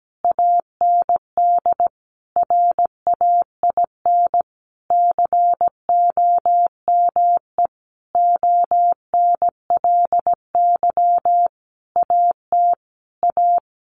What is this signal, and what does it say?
Morse code, heard as a keyed tone, about 17 wpm, 700 Hz. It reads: AND RAIN COME ONLY AT A
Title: Sentences from Top 1400 Words 17wpm